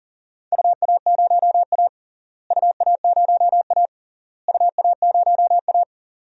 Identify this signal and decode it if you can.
Morse code, heard as a keyed tone, about 40 wpm, 700 Hz. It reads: UA0A UA0A UA0A
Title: DXCC-100 Call Signs - 4 Characters - Repeated 3x - 40wpm